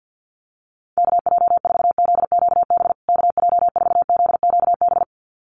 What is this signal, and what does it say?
Morse code, heard as a keyed tone, about 50 wpm, 700 Hz. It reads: KJ4ZQB KJ4ZQB
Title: USA Call Signs Repeated 2x 50wpm